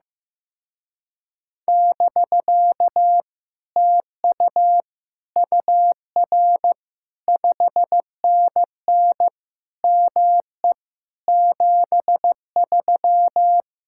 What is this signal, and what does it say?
Morse code, heard as a keyed tone, about 15 wpm, 700 Hz. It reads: <BK> TU UR 5NN ME 73
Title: Speed-Racing Common QSO Phrases Encore - 15wpm